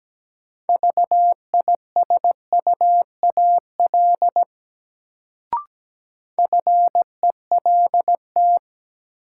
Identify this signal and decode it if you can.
Morse code, heard as a keyed tone, about 17 wpm, 700 Hz. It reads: VISUAL FELT
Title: Sets of 5 Words Encore - New Word Drill - 17wpm